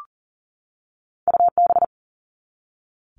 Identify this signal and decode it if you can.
Morse code, heard as a keyed tone, about 37 wpm, 700 Hz. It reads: UB
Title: All Cut Numbers - 2-Digit 40wpm